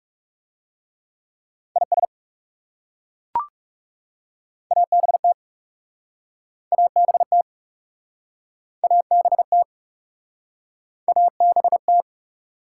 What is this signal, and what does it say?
Morse code, heard as a keyed tone, about 40 wpm, 700 Hz. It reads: IS ABT ABT ABT ABT